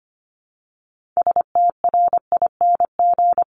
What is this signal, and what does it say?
Morse code, heard as a keyed tone, about 25 wpm, 700 Hz. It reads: STRING
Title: Multi-letter Phonemes 25wpm